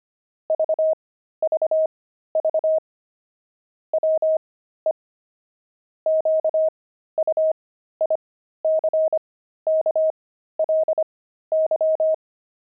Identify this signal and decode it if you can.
Morse code, heard as a keyed tone, about 25 wpm, spaced as for 13 wpm, 630 Hz. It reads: VVV WE QUICKLY
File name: pangram-fs-13-cs-25-axle